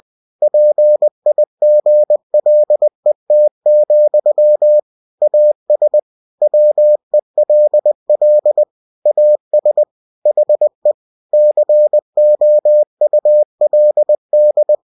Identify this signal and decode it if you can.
Morse code, heard as a keyed tone, about 20 wpm, 600 Hz. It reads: PIGLET, AS WELL AS HE COULD
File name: cw-books / winnie-the-pooh / winnie-20wpm-page-0071